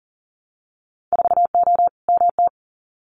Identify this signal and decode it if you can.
Morse code, heard as a keyed tone, about 42 wpm, 700 Hz. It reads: 4O MT
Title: Winter Field Day Exchanges Encore 40wpm